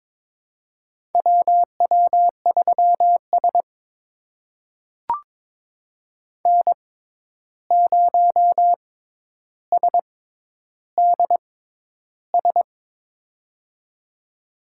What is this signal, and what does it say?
Morse code, heard as a keyed tone, about 22 wpm, spaced as for 11 wpm, 700 Hz. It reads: WW3S N0SDS
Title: USA Call Signs - 6x Character Spacing - 22wpm